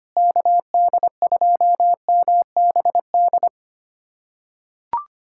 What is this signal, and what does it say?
Morse code, heard as a keyed tone, about 25 wpm, 700 Hz. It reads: KD2MBD
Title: USA Callsign DE Callsign - 6 Characters - 25wpm